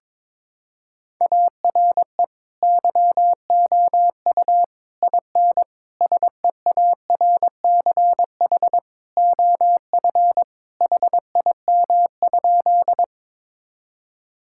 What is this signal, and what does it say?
Morse code, heard as a keyed tone, about 22 wpm, 700 Hz. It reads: ARE YOU IN SEARCH OF HIM?
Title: Sentences from Top 1000 Words - Review Part 5 - 22wpm